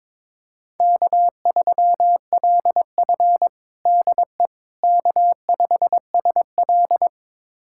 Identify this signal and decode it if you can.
Morse code, heard as a keyed tone, about 22 wpm, 700 Hz. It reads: K3LF DE K5SL
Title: USA Callsign DE Callsign - 4 Characters - 22wpm